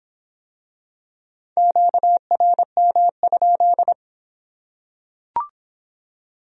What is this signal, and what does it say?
Morse code, heard as a keyed tone, about 22 wpm, 700 Hz. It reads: QRM?